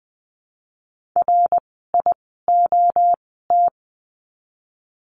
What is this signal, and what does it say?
Morse code, heard as a keyed tone, about 20 wpm, 700 Hz. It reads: R I O T
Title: 061 Lesson 11f - Letter h - Words 20wpm